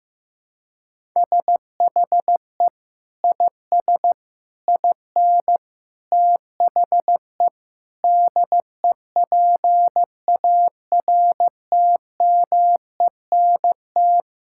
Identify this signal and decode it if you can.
Morse code, heard as a keyed tone, about 15 wpm, 700 Hz. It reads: SHE IS IN THE DEPARTMENT
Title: Sentences from Top 1000 Words - Review Part 5 - 15wpm